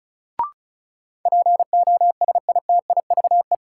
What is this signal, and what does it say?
Morse code, heard as a keyed tone, about 36 wpm, 700 Hz. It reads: POSITIVE
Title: Sentences from Top 1400 Words - New Word Drill Alt - 35wpm